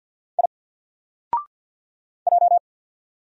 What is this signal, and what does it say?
Morse code, heard as a keyed tone, about 50 wpm, 700 Hz. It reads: I J
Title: Single Letter-Number 50wpm